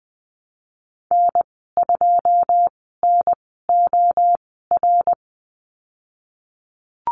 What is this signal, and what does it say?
Morse code, heard as a keyed tone, about 20 wpm, spaced as for 15 wpm, 700 Hz. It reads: N2NOR
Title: 088 Lesson 15g - Number 5 - Call Signs 20wpm